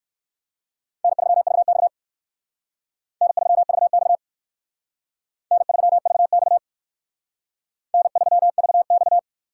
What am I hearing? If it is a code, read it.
Morse code, heard as a keyed tone, about 59 wpm, 700 Hz. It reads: N3VX N3VX N3VX N3VX